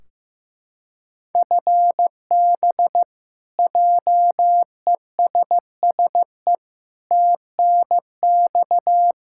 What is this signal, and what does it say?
Morse code, heard as a keyed tone, about 15 wpm, 700 Hz. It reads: FB JESSE TNX
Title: Practice QSO - Exchange 3 - More Info - 15wpm